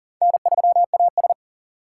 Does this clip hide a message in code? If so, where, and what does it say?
Yes: Morse code from 0.2 to 1.3 s, heard as a keyed tone, about 52 wpm, 700 Hz. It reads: N3AS